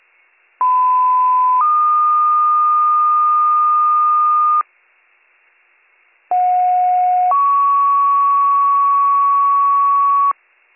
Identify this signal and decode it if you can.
Two-tone page, heard as a one-second tone, then a three-second tone, about 1010 then 1220 Hz, about 730 then 1120 Hz. It is heard from 0.6 to 10.3 s.